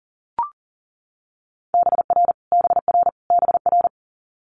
Morse code, heard as a keyed tone, about 40 wpm, 700 Hz. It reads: BR BR BR
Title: Multi-letter Phonemes Repeated 3x 40wpm